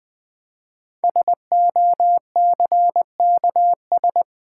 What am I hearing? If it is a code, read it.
Morse code, heard as a keyed tone, about 20 wpm, 700 Hz. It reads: SOCKS
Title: Easy Words with Hints - 20wpm